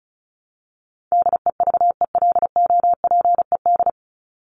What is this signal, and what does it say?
Morse code, heard as a keyed tone, about 34 wpm, 700 Hz. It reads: DEVELOPED
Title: ED Suffix - Top 100 Words with Base Words 35wpm